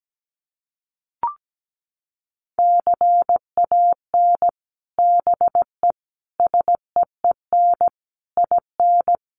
Morse code, heard as a keyed tone, about 17 wpm, 700 Hz. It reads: CAN BE SEEN IN
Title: Sets of 5 Words Encore2 17wpm